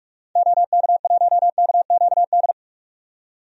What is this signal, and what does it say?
Morse code, heard as a keyed tone, about 46 wpm, 700 Hz. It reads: OK1KQD
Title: DXCC-100 Call Signs - 6 Characters - 45wpm